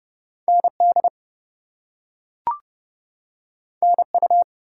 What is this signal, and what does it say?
Morse code, heard as a keyed tone, about 30 wpm, 700 Hz. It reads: ND NU
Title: US States and Canadian Provinces 30wpm